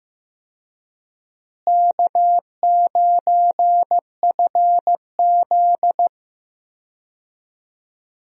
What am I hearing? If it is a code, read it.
Morse code, heard as a keyed tone, about 15 wpm, 700 Hz. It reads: K9FZ
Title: USA Call Signs - 4 Characters - Repeated 2x - 15wpm